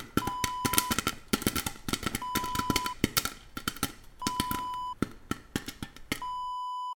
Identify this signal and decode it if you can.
There is no code.